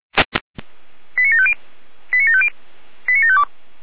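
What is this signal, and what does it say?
Five-tone selective call (ZVEI), heard as quick runs of five tones, tones of about 70 ms. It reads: 89750 89750 89752
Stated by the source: ZVEI five-tone call received with an SDR.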